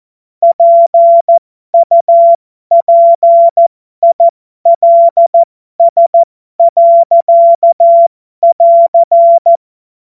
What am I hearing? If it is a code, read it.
Morse code, heard as a keyed tone, about 14 wpm, spaced as for 12 wpm, 680 Hz. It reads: PUPILS.<AR>
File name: Morsekurs / Witze